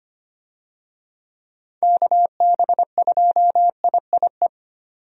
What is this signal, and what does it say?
Morse code, heard as a keyed tone, about 25 wpm, 700 Hz. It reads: KB2IIE
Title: USA Call Signs - 6 Characters - 25wpm